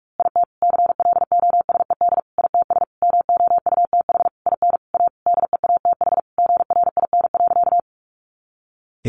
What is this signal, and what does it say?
Morse code, heard as a keyed tone, about 44 wpm, 700 Hz. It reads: IT CLOSED ITS MOUTH IN A DEATH GRIN.
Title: A Sound Of Thunder 45wpm